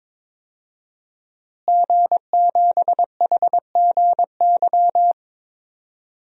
Morse code, heard as a keyed tone, about 22 wpm, 700 Hz. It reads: G7HGY
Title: DXCC-100 Call Signs - 5 Characters - Repeated 2x - 22wpm